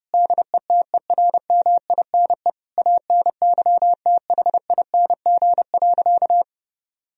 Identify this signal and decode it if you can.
Morse code, heard as a keyed tone, about 30 wpm, 700 Hz. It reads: DETERMINE ANYTHING.
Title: Sentences from Top 1000 Words - Review Part 4 - 30wpm